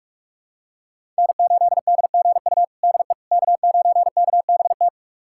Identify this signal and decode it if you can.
Morse code, heard as a keyed tone, about 42 wpm, 700 Hz. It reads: N8DGU DE K9KBT
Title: USA Callsign DE Callsign - 5 Characters - 45wpm